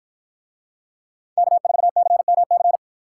Morse code, heard as a keyed tone, about 53 wpm, 700 Hz. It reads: K4CMC